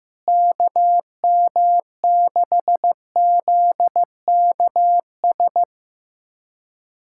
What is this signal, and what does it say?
Morse code, heard as a keyed tone, about 15 wpm, 700 Hz. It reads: KM6ZKS
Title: USA Callsign DE Callsign 15wpm